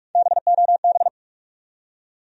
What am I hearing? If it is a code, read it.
Morse code, heard as a keyed tone, about 43 wpm, 700 Hz. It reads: BOB
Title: USA Common Names - Max 5 Letters - 45wpm